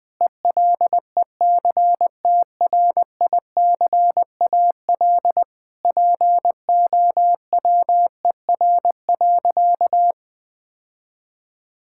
Morse code, heard as a keyed tone, about 21 wpm, 700 Hz. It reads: ELECTRICAL POWER.
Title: Sentences from Top 1000 Words - Review Part 6 - 20wpm